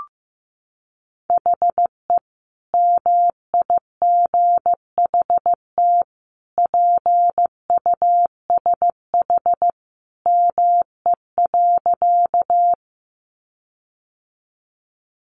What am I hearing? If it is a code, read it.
Morse code, heard as a keyed tone, about 15 wpm, 700 Hz. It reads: HE MIGHT PUSH ME.
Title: Sentences from Top 1300 Words 15wpm